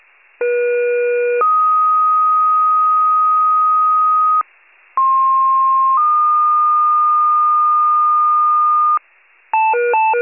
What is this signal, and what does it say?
Two-tone page, heard as a one-second tone, then a three-second tone, about 480 then 1250 Hz, about 1060 then 1220 Hz. It is heard from 0.4 to 9.0 s.
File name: pager / 23 FIRE / wcalert